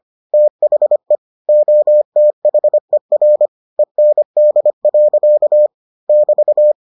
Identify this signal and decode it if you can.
Morse code, heard as a keyed tone, about 25 wpm, 600 Hz. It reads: THE OTHER END. <BT>